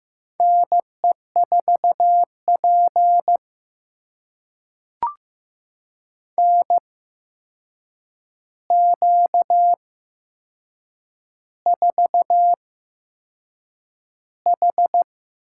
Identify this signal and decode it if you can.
Morse code, heard as a keyed tone, about 15 wpm, spaced as for 6 wpm, 700 Hz. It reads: NE4P NQ4H
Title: USA Call Signs - 4 Characters - 8x Character Spacing - 15wpm